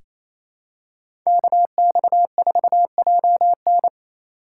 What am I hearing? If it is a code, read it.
Morse code, heard as a keyed tone, about 28 wpm, 700 Hz. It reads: KX4JN
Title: USA Call Signs - 5 Characters - 2x Character Spacing - 28wpm